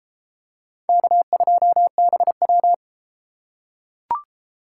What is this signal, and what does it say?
Morse code, heard as a keyed tone, about 34 wpm, 700 Hz. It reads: K2BW